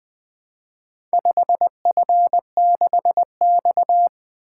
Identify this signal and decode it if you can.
Morse code, heard as a keyed tone, about 20 wpm, 700 Hz. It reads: 5F6X
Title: Four Letter-Number Combinations - 20wpm